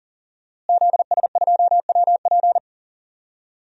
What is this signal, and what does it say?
Morse code, heard as a keyed tone, about 40 wpm, 700 Hz. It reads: ZS2WP
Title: DXCC-100 Call Signs - 5 Characters - Repeated 3x - 40wpm